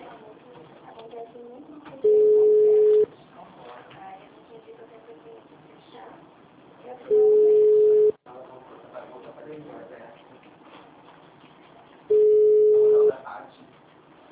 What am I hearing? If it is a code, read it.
Ringback tone (Europe, 425 Hz), with speech.
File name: out- -2006- - - .3767